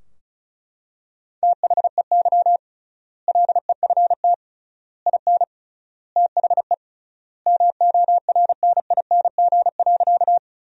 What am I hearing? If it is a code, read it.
Morse code, heard as a keyed tone, about 34 wpm, 700 Hz. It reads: THEY LEFT IN THE MORNING.
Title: 3x Word-Spacing - Sentences from Top 400 Words - 35wpm